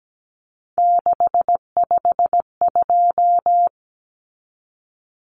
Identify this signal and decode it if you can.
Morse code, heard as a keyed tone, about 17 wpm, 700 Hz. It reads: 652
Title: Common Cut Numbers - 3-Digit 17wpm